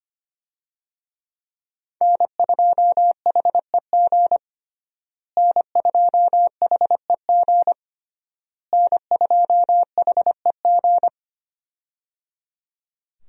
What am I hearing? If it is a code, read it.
Morse code, heard as a keyed tone, about 25 wpm, 700 Hz. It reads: N2HEG N2HEG N2HEG
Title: USA Call Signs - 5 Characters - Repeated 3x - 25wpm